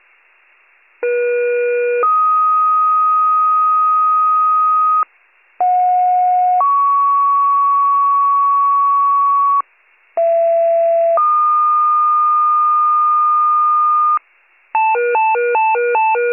Two-tone page, heard as a one-second tone, then a three-second tone, about 480 then 1250 Hz, about 730 then 1120 Hz, about 670 then 1220 Hz. It is heard from 1.0 to 14.2 s.